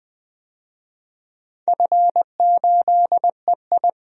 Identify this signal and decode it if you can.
Morse code, heard as a keyed tone, about 20 wpm, 700 Hz. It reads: F8EI
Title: DXCC-100 Call Signs - 4 Characters - 20wpm